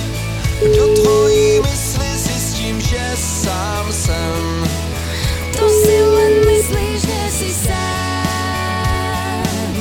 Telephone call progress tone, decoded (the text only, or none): ringback tone